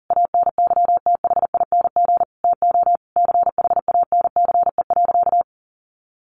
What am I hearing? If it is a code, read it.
Morse code, heard as a keyed tone, about 40 wpm, 700 Hz. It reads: ANYTHING TO CHANCE.
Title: Sentences from Top 1100 Words 40wpm